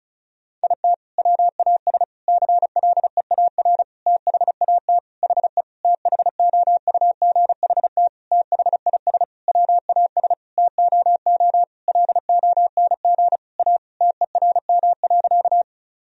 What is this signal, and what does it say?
Morse code, heard as a keyed tone, about 34 wpm, 700 Hz. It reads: IT WAS CLEAR THAT HE THOUGHT THIS WAS TOO LONG A TERM.
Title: Sentences from Top 1000 Words - Review Part 4 - 35wpm